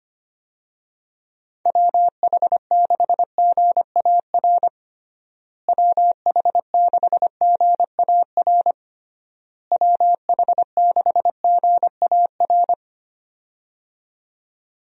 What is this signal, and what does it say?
Morse code, heard as a keyed tone, about 25 wpm, 700 Hz. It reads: WH6GAR WH6GAR WH6GAR
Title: USA Call Signs - 6 Characters - Repeated 3x - 25wpm